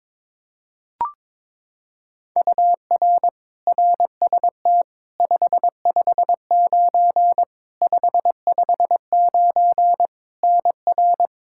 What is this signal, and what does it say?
Morse code, heard as a keyed tone, about 22 wpm, 700 Hz. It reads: UR RST 559 559 NR
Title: K1USN SST QSO Practice 22wpm